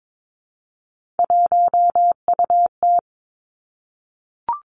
Morse code, heard as a keyed tone, about 22 wpm, 700 Hz. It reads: 1UT
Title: Three Challenging Letter-Number Combinations 22wpm